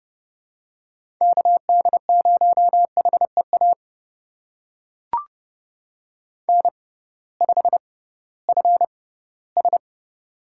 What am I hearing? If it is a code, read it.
Morse code, heard as a keyed tone, about 32 wpm, spaced as for 18 wpm, 700 Hz. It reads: KD0HEA N5FS
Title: USA Call Signs - 6x Character Spacing - 30wpm